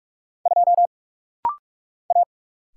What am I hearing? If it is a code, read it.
Morse code, heard as a keyed tone, about 44 wpm, 700 Hz. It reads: 2 A
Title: Single Letter-Number Alt - Rapid-Fire 45wpm